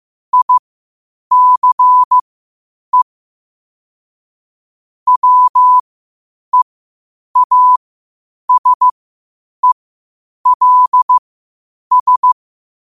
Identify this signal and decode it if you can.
Morse code, heard as a keyed tone, about 15 wpm, spaced as for 8 wpm, 1000 Hz. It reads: ICE WEASELS
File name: qotdfiles / 8wpm